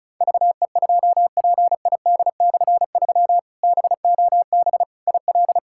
Morse code, heard as a keyed tone, about 36 wpm, 700 Hz. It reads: VE2PID/3 BOB IL
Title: K1USN SST QSO Practice - 35wpm